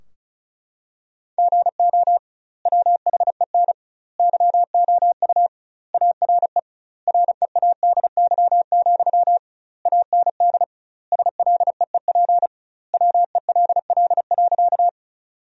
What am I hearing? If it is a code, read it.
Morse code, heard as a keyed tone, about 36 wpm, 700 Hz. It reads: GO WHEN YOU ARE READY, AND SLEEP WELL.
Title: 2x Word-Spacing - Sentences from Top 500 Words - 35wpm